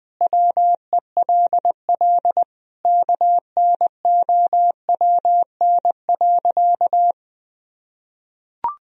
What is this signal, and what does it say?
Morse code, heard as a keyed tone, about 21 wpm, 700 Hz. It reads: WELL KNOWN.
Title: Sentences from Top 1000 Words - Review Part 4 - 20wpm